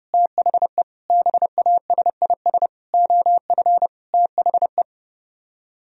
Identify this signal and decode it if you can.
Morse code, heard as a keyed tone, about 36 wpm, 700 Hz. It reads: THE BASIS OF THE
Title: Speed-Racing - Sets of 5 Words 30wpm